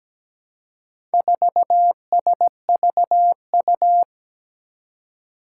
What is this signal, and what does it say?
Morse code, heard as a keyed tone, about 17 wpm, 700 Hz. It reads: 4SVU
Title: Four Challenging Letter-Number Combinations 17wpm